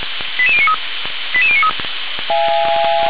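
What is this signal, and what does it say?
Five-tone selective call (ZVEI), heard as quick runs of five tones, tones of about 80 ms. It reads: 88093 88093
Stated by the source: ZVEI five-tone selective call received via SDR, fire brigade Feuerwehr Orlamünde.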